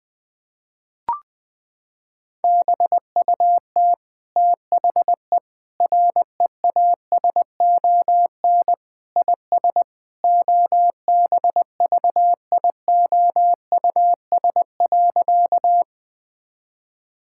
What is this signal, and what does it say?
Morse code, heard as a keyed tone, about 19 wpm, 700 Hz. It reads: BUT THE REASON IS OBVIOUS.
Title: Sentences from Top 1600 Words 20wpm